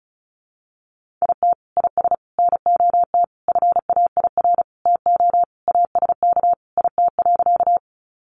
Morse code, heard as a keyed tone, about 34 wpm, 700 Hz. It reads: IT IS NOT FAIR TO ASK IT.
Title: Sentences from Top 1300 Words 35wpm